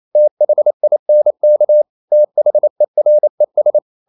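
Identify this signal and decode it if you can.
Morse code, heard as a keyed tone, about 28 wpm, 600 Hz. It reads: THINK THERES
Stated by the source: Morse code practice at 28 wpm, text of Ken Follett's novel Never in English, from CW med Gnister.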